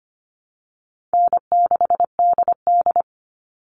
Morse code, heard as a keyed tone, about 25 wpm, 700 Hz. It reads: N6DD
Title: All Cut Numbers - 4-Digit 25wpm